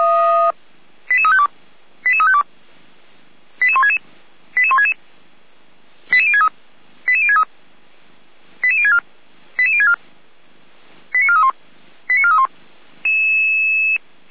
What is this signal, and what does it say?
Five-tone selective call (ZVEI), heard as quick runs of five tones, tones of about 70 ms. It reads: 88352 88352 88166 88166 88063 88063 88064 88064 89431 89431